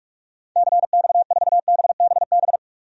Morse code, heard as a keyed tone, about 46 wpm, 700 Hz. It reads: CX4BBB
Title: DXCC-100 Call Signs - 6 Characters - 45wpm